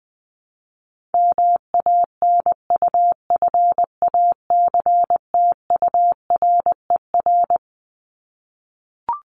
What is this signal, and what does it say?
Morse code, heard as a keyed tone, about 20 wpm, 700 Hz. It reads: MANUFACTURER
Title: Top 1751-2000 Words 20wpm